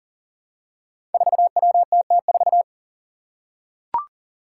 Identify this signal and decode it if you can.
Morse code, heard as a keyed tone, about 40 wpm, 700 Hz. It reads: 4WTT4
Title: Five Challenging Letter-Number Combinations 40wpm